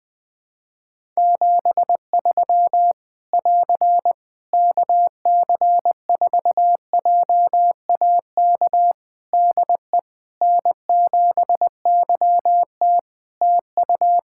Morse code, heard as a keyed tone, about 20 wpm, 700 Hz. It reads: 73 <AR> KC4JAK DE N7YT TU
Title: Practice QSO - Exchange 4 - Ending - 20wpm